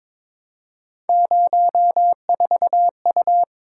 Morse code, heard as a keyed tone, about 22 wpm, 700 Hz. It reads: 04U
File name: practice / 22wpm / Three Letter-Number Combinations - 22wpm